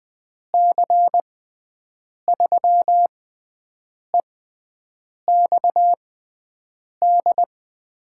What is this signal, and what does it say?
Morse code, heard as a keyed tone, about 21 wpm, spaced as for 7 wpm, 700 Hz. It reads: C3EXD
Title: USA Call Signs - 6 Characters - 6x Character Spacing - 20wpm